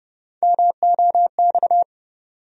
Morse code, heard as a keyed tone, about 32 wpm, 700 Hz. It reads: MOX
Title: USA Call Sign Postfixes - 3 Letters 30wpm